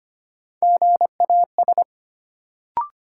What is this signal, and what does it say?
Morse code, heard as a keyed tone, about 25 wpm, 700 Hz. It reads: GAS
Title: Sentences from Top 1100 Words - New Word Drill - 25wpm